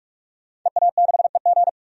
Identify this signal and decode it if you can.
Morse code, heard as a keyed tone, about 46 wpm, 700 Hz. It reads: EA6EG
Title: DXCC-100 Call Signs - 5 Characters - 45wpm